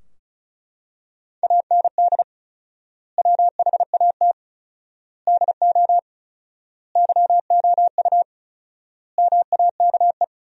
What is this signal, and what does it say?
Morse code, heard as a keyed tone, about 34 wpm, 700 Hz. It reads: AND WHAT DO YOU MAKE
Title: 4x Word-Spacing - Sentences from Top 100 Words - 35wpm